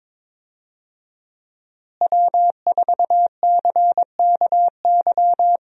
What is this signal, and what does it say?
Morse code, heard as a keyed tone, about 21 wpm, 700 Hz. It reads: W4CKY
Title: USA Call Signs - 5 Characters - Repeated 3x - 22wpm